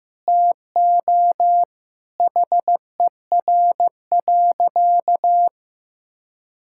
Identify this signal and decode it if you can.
Morse code, heard as a keyed tone, about 16 wpm, 700 Hz. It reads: TO HER.
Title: Sentences from Top 1000 Words - Review Part 6 - 15wpm